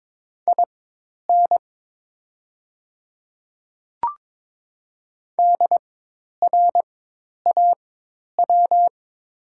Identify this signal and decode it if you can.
Morse code, heard as a keyed tone, about 22 wpm, spaced as for 10 wpm, 700 Hz. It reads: IN DRAW
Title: 4x Character Spacing - Top 300 Words - 22wpm